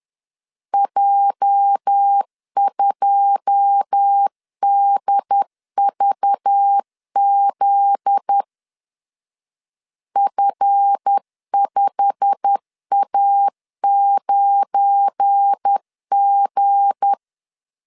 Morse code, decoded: J2DVZ F5A9G

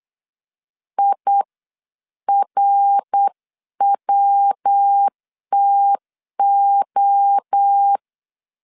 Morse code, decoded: IRWTO

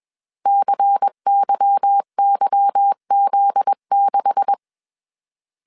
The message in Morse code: CYYZ6

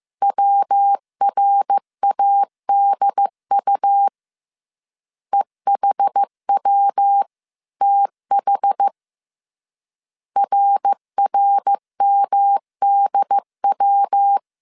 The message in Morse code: WRADU EHWTH RRMDW

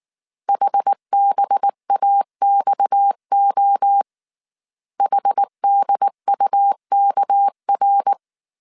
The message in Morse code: HBAXO HDUKR

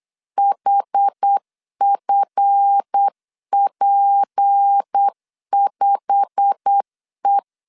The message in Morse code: HFP5E

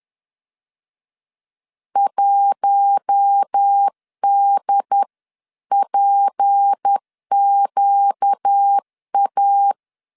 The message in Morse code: 1DPQA

